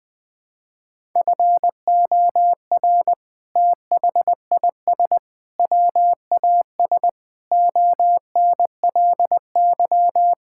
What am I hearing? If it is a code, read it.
Morse code, heard as a keyed tone, about 21 wpm, 700 Hz. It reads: FOR THIS WAS ONLY
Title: Sentences from Top 1000 Words - Review Part 5 - 20wpm